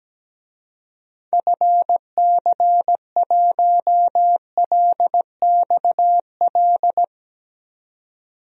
Morse code, heard as a keyed tone, about 17 wpm, 700 Hz. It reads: FC1LXL
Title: DXCC-100 Call Signs - 6 Characters - 17wpm